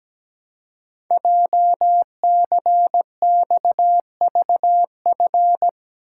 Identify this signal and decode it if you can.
Morse code, heard as a keyed tone, about 17 wpm, 700 Hz. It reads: JCXVF
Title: Code Groups - 5 Letters - 17wpm